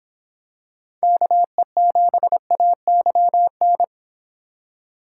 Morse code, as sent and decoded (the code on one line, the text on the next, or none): -.- . --... .- -.-- -.
KE7AYN